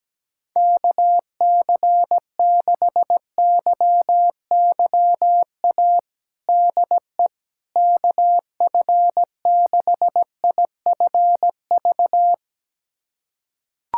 Morse code, heard as a keyed tone, about 17 wpm, 700 Hz. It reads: KC6YYA DE KF6IFV
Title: USA Callsign DE Callsign - 6 Characters - 17wpm